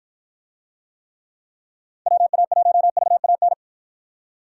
Morse code, heard as a keyed tone, about 52 wpm, 700 Hz. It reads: WA1VAN